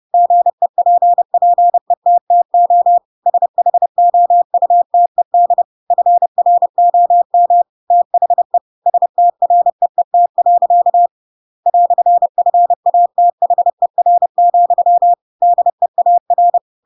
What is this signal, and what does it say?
Morse code, heard as a keyed tone, about 30 wpm, 700 Hz. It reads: GEPPETTO SHOUTED FROM THE STREET. "FATHER, DEAR
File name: Pinocchio / Pinocchio ch7 30 30 ik1hge